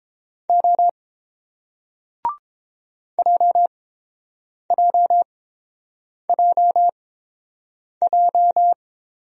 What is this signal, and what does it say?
Morse code, heard as a keyed tone, about 29 wpm, spaced as for 7 wpm, 690 Hz. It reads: O JJJJ